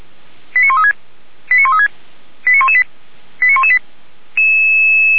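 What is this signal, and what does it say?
Five-tone selective call (ZVEI), heard as quick runs of five tones, tones of about 70 ms. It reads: 89136 89136 89118 89118
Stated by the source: ZVEI five-tone selective call received on an SDR.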